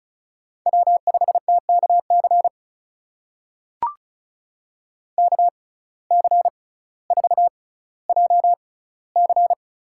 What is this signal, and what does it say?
Morse code, heard as a keyed tone, about 36 wpm, spaced as for 18 wpm, 700 Hz. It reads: W5TKC KC4JC
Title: USA Call Signs - 5 Characters - 6x Character Spacing - 35wpm